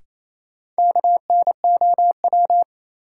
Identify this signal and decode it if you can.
Morse code, heard as a keyed tone, about 28 wpm, 700 Hz. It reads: KNOW
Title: Sets of 2 Words - New Word Drill - 28wpm